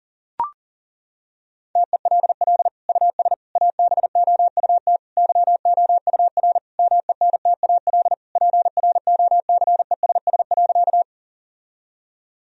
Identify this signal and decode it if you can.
Morse code, heard as a keyed tone, about 37 wpm, 700 Hz. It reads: TELL US ABOUT YOUR MENTAL PROCESS.
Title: Sentences from Top 1000 Words - Review Part 4 - 40wpm